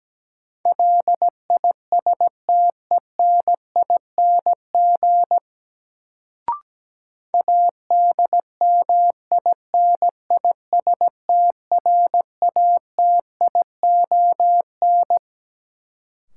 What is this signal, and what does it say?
Morse code, heard as a keyed tone, about 17 wpm, 700 Hz. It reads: LISTENING ADMINISTRATION
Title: Sets of 5 Words - New Word Drill - 17wpm